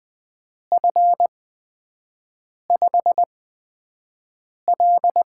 Morse code, heard as a keyed tone, about 20 wpm, spaced as for 5 wpm, 700 Hz. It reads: F5L
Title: USA Call Signs - 4 Characters - 8x Character Spacing - 20wpm